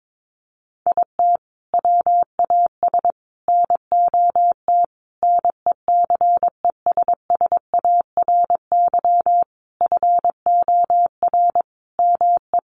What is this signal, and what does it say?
Morse code, heard as a keyed tone, about 22 wpm, 700 Hz. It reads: IT WAS NOT NECESSARY FOR ME
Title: Sentences from Top 1300 Words 22wpm